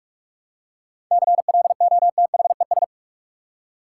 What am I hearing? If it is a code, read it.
Morse code, heard as a keyed tone, about 42 wpm, 700 Hz. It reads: CLOTHES
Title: Seven Letter Words - 45wpm